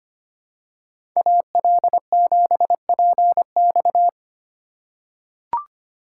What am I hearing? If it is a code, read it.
Morse code, heard as a keyed tone, about 25 wpm, 700 Hz. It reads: AL7PX
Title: USA Call Signs - 3x Character Spacing - 25wpm